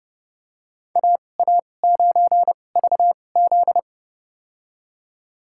Morse code, heard as a keyed tone, about 31 wpm, spaced as for 22 wpm, 700 Hz. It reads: AA9VZ
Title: USA Call Signs - 2x Character Spacing - 30wpm